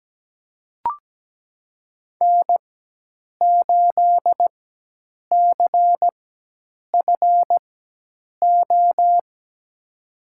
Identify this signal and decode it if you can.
Morse code, heard as a keyed tone, about 17 wpm, spaced as for 8 wpm, 700 Hz. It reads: N8CFO
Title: USA Call Signs - 5 Characters - 4x Character Spacing - 17wpm